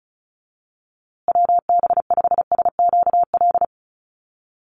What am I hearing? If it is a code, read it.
Morse code, heard as a keyed tone, about 34 wpm, 700 Hz. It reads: WB5SQL
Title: USA Call Signs Repeated 3x 35wpm